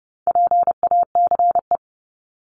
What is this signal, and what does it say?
Morse code, heard as a keyed tone, about 30 wpm, 700 Hz. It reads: PACE
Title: Top 1751-2000 Words 30wpm